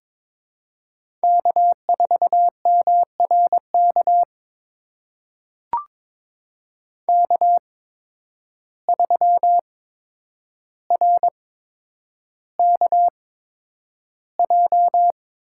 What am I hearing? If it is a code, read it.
Morse code, heard as a keyed tone, about 23 wpm, spaced as for 9 wpm, 700 Hz. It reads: K4MRK K3RKJ
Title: USA Call Signs - 8x Character Spacing - 22wpm